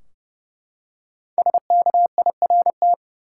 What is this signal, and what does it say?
Morse code, heard as a keyed tone, about 30 wpm, 700 Hz. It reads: SKIRT
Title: Five Letter Words - 30wpm